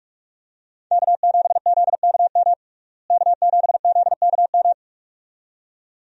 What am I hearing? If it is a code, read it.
Morse code, heard as a keyed tone, about 41 wpm, 700 Hz. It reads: K7ZKM K7ZKM
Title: USA Call Signs - 5 Characters - Repeated 2x - 45wpm